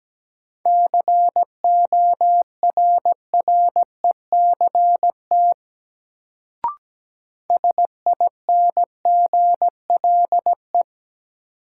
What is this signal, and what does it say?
Morse code, heard as a keyed tone, about 17 wpm, 700 Hz. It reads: CORRECT SINGLE
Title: Sentences from Top 700 Words - New Word Drill - 17wpm